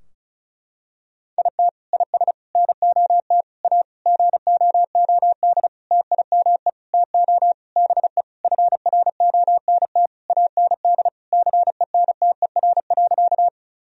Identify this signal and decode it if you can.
Morse code, heard as a keyed tone, about 34 wpm, 700 Hz. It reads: IT IS NOT A GOOD TIME TO BE FRONT AND CENTER.
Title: Sentences from Top 1000 Words - Review Part 2 - 35wpm